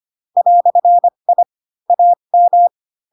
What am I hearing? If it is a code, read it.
Morse code, heard as a keyed tone, about 25 wpm, spaced as for 22 wpm, 700 Hz. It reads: "I AM
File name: Pinocchio / Pinocchio ch4 18 25 ik1hge